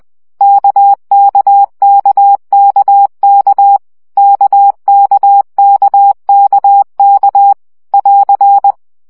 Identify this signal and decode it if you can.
Morse code, heard as a keyed tone, about 20 wpm, 800 Hz. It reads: KKKKK KKKKK <AR>
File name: CW gyakorlas k 17 baud